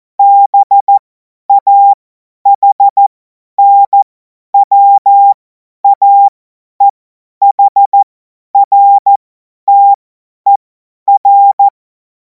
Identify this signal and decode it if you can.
Morse code, heard as a keyed tone, about 14 wpm, spaced as for 10 wpm, 800 Hz. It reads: BAHNWAEHRTER